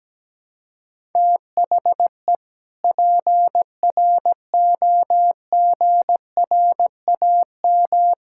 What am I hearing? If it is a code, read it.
Morse code, heard as a keyed tone, about 17 wpm, 700 Hz. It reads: THE PROGRAM
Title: Sentences from Top 1000 Words - Review Part 5 - 17wpm